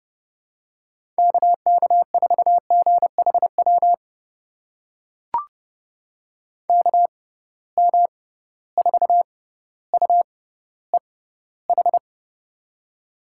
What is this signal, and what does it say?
Morse code, heard as a keyed tone, about 30 wpm, spaced as for 15 wpm, 700 Hz. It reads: KK4GHW KM4UEH
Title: USA Call Signs - 6 Characters - 6x Character Spacing - 30wpm